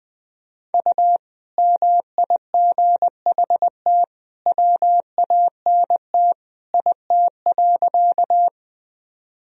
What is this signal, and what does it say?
Morse code, heard as a keyed tone, about 21 wpm, 700 Hz. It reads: U MIGHT WANT IT.
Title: Sentences from Top 300 Words and Abbreviations 20wpm